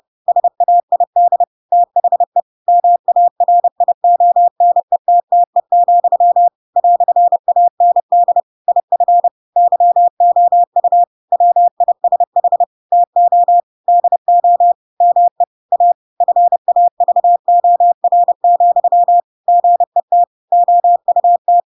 Morse code, heard as a keyed tone, about 30 wpm, 700 Hz. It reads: SAID THE MARIONETTE, "AND IF YOU WISH TO DO ME A FAVOR, GET OUT
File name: Pinocchio / Pinocchio ch4 30 30 ik1hge